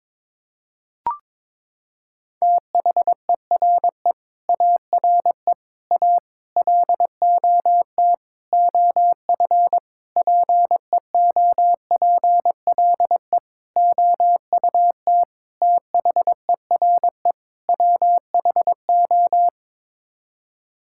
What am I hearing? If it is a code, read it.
Morse code, heard as a keyed tone, about 21 wpm, 700 Hz. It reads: THERE ARE A LOT OF PEOPLE OUT THERE WHO
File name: Sets of 9 Words - 22wpm